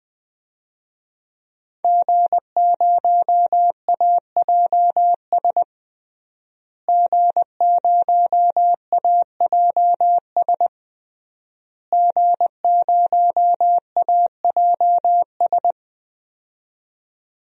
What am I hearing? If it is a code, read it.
Morse code, heard as a keyed tone, about 20 wpm, 700 Hz. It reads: G0AJS G0AJS G0AJS
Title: DXCC-100 Call Signs - 5 Characters - Repeated 3x - 20wpm